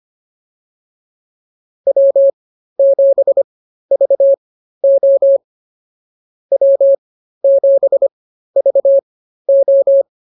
Morse code, heard as a keyed tone, about 25 wpm, spaced as for 13 wpm, 550 Hz. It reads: W7VO W7VO